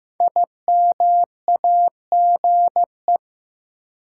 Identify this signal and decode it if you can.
Morse code, heard as a keyed tone, about 15 wpm, 700 Hz. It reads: IMAGE
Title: Sentences from Top 700 Words - New Word Drill - 15wpm